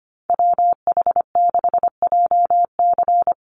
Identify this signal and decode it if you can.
Morse code, heard as a keyed tone, about 25 wpm, 700 Hz. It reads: WH6JC
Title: USA Call Signs 25wpm